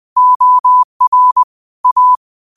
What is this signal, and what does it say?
Morse code, heard as a keyed tone, about 20 wpm, 1000 Hz. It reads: OR A
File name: qotdfiles / 20wpm